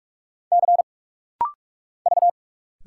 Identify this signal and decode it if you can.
Morse code, heard as a keyed tone, about 45 wpm, 700 Hz. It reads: C V
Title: Single Letter-Number - Rapid-Fire 45wpm